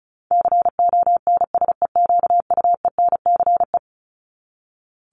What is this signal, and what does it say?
Morse code, heard as a keyed tone, about 34 wpm, 700 Hz. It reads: CONSEQUENCE
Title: LY Suffix - Top 100 Words with Base Words 35wpm